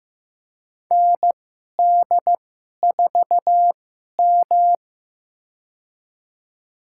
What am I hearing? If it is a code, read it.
Morse code, heard as a keyed tone, about 15 wpm, spaced as for 11 wpm, 700 Hz. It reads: ND4M
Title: USA Call Signs - 2x Character Spacing - 15wpm